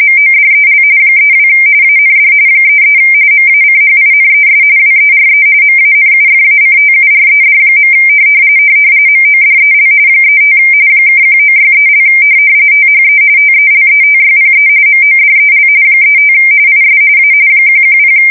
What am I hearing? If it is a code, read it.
RTTY, about 45.45 baud. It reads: MODEL 32 AND 33 MACHINES WERE MANUFACTURED. MODEL 35 (1963) THE TELETYPE MODEL 35 IS A 110
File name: fsk170 / teletype-corp